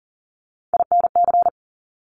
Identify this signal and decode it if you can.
Morse code, heard as a keyed tone, about 38 wpm, 690 Hz. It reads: INC
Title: Top Three Letter Combinations in Words 40wpm